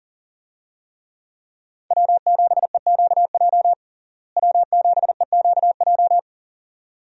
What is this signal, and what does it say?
Morse code, heard as a keyed tone, about 40 wpm, 700 Hz. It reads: W7EQJ W7EQJ
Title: USA Call Signs - 5 Characters - Repeated 2x - 40wpm